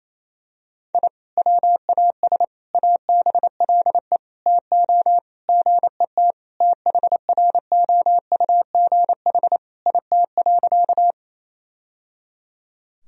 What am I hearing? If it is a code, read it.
Morse code, heard as a keyed tone, about 28 wpm, 700 Hz. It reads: I WAS ABLE TO GET THROUGH IT.
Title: Sentences from Top 1000 Words - Review Part 2 - 28wpm